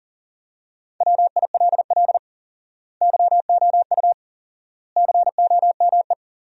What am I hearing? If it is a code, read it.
Morse code, heard as a keyed tone, about 40 wpm, 700 Hz. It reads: WILL YOU COME
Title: 4x Word-Spacing - Sentences from Top 200 Words - 40wpm